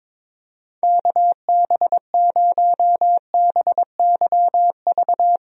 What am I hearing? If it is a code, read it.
Morse code, heard as a keyed tone, about 22 wpm, 700 Hz. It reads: KB0BYV
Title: USA Call Signs - 6 Characters - Repeated 3x - 22wpm